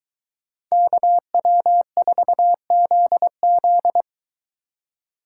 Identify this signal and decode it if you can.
Morse code, heard as a keyed tone, about 23 wpm, 700 Hz. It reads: KW4ZZ